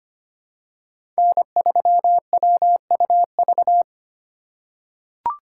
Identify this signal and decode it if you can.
Morse code, heard as a keyed tone, about 25 wpm, 700 Hz. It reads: N3WUV
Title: USA Call Signs - 5 Characters - 2x Character Spacing - 25wpm